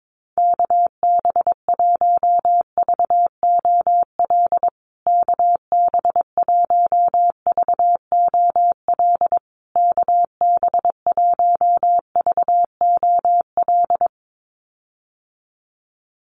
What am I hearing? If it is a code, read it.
Morse code, heard as a keyed tone, about 22 wpm, 700 Hz. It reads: KB1VOL KB1VOL KB1VOL
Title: USA Call Signs Repeated 3x 22wpm